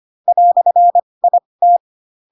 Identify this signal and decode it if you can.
Morse code, heard as a keyed tone, about 25 wpm, spaced as for 20 wpm, 700 Hz. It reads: "IT
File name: Pinocchio / Pinocchio ch1 15 25 ik1hge